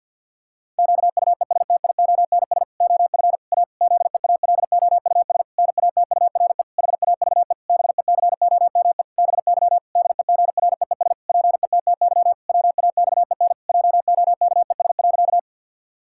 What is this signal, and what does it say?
Morse code, heard as a keyed tone, about 46 wpm, 700 Hz. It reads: QUESTIONS OF A ZEALOUS NATURE HAVE BECOME BY DEGREES PETTY WAXEN JOKES.
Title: Pangram Challenge v2 - 50wpm